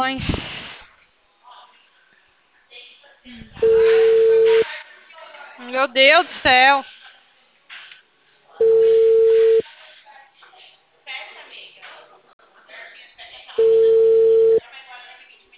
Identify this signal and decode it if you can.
Ringback tone (China, 450 Hz), with speech.
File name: out- -2004- - - .24